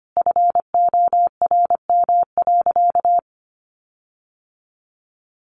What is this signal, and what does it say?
Morse code, heard as a keyed tone, about 25 wpm, 700 Hz. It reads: FORM.
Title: Sentences from Top 1500 Words 25wpm